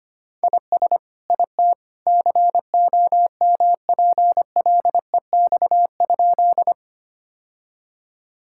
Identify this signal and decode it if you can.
Morse code, heard as a keyed tone, about 25 wpm, 700 Hz. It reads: IS IT COMPLEX?
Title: Sentences from Top 1000 Words - Review Part 6 - 25wpm